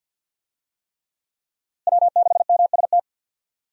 Morse code, heard as a keyed tone, about 49 wpm, 700 Hz. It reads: W6MST
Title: USA Call Signs - 5 Characters - Repeated 3x - 50wpm